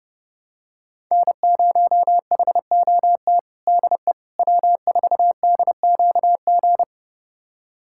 Morse code, heard as a keyed tone, about 30 wpm, 700 Hz. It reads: N0HOT DE W4DQG
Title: USA Callsign DE Callsign - 5 Characters - 30wpm